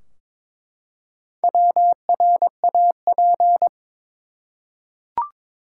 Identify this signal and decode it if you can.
Morse code, heard as a keyed tone, about 21 wpm, 700 Hz. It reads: WRAP
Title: Four Letter Words - 22wpm